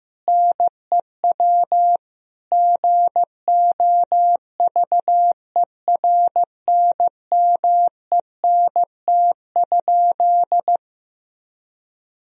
Morse code, heard as a keyed tone, about 16 wpm, 700 Hz. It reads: NEW GOVERNMENT?
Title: Sentences from Top 1000 Words - Review Part 3 - 15wpm